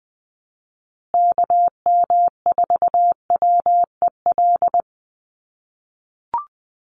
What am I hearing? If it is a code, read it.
Morse code, heard as a keyed tone, about 20 wpm, 700 Hz. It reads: KM4WEL
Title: USA Call Signs Repeated 2x 20wpm